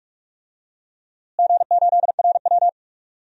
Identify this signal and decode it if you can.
Morse code, heard as a keyed tone, about 42 wpm, 700 Hz. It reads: G8RW
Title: DXCC-100 Call Signs - 4 Characters - Repeated 2x - 45wpm